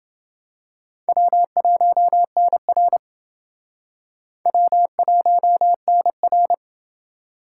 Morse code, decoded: W1NR W1NR

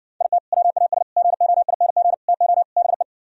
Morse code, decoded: IT WAS GOING TO BE

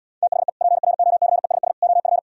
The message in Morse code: THE PROCESS OF

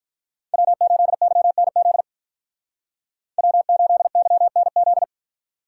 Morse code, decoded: W8YNZ W8YNZ

